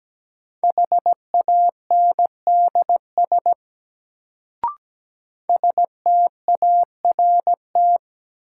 Morse code, heard as a keyed tone, about 17 wpm, 700 Hz. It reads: HANDS START
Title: Sets of 4 Words - New Word Drill - 17wpm